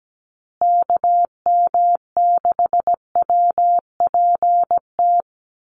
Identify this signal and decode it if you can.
Morse code, heard as a keyed tone, about 17 wpm, 700 Hz. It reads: KM6WPT
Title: USA Call Signs Repeated 3x 17wpm